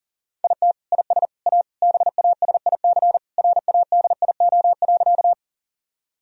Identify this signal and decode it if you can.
Morse code, heard as a keyed tone, about 40 wpm, 700 Hz. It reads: IT IS A BASIC RADIO.
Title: Sentences from Top 1000 Words - Review Part 3 - 40wpm